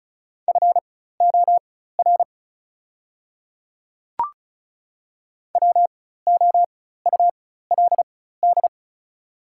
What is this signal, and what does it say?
Morse code, heard as a keyed tone, about 36 wpm, spaced as for 16 wpm, 690 Hz. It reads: FOR WOULD
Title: 4x Character Spacing - Top 100 Words - 35wpm